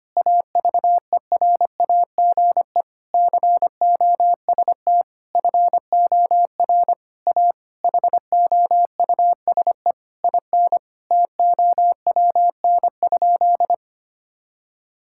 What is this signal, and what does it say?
Morse code, heard as a keyed tone, about 25 wpm, 700 Hz. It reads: AVERAGE COST FOR A HOUSE IN TOWN?
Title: Sentences from Top 1000 Words - Review Part 6 - 25wpm